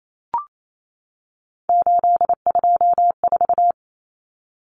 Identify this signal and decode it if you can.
Morse code, heard as a keyed tone, about 28 wpm, 700 Hz. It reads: 824
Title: Common Cut Numbers - 3-Digit 28wpm